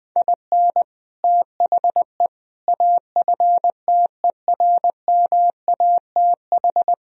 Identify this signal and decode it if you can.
Morse code, heard as a keyed tone, about 20 wpm, 700 Hz. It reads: IN THE AFTERMATH
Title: Speed-Racing - Sets of 5 Words 20wpm